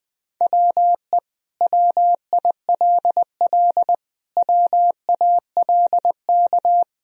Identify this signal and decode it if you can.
Morse code, heard as a keyed tone, about 20 wpm, 700 Hz. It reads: WE WILL WALK
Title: Sentences from Top 1000 Words - Review Part 1 - 20wpm